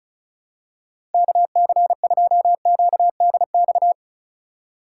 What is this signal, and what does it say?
Morse code, heard as a keyed tone, about 36 wpm, 700 Hz. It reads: KC2QDX
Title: USA Call Signs - 4x Character Spacing - 35wpm